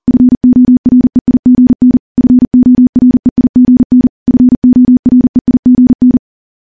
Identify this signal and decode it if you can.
Morse code, heard as a keyed tone, about 40 wpm, 250 Hz. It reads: FOREIGN FOREIGN FOREIGN
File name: morse / morse-training-40WPM